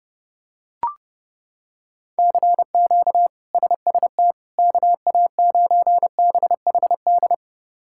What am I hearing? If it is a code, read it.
Morse code, heard as a keyed tone, about 32 wpm, 700 Hz. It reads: CQ SST KA9BHD
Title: K1USN SST QSO Practice - 30wpm